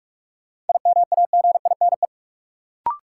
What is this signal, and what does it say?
Morse code, heard as a keyed tone, about 45 wpm, 700 Hz. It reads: IMAGINE
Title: Sentences from Top 1400 Words - New Word Drill Alt - 45wpm